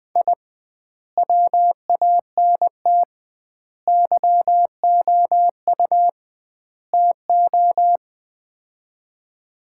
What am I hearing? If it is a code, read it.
Morse code, heard as a keyed tone, about 20 wpm, 700 Hz. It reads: I WANT YOU TO
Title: 2x Word-Spacing - Sets of 4 Words - 20wpm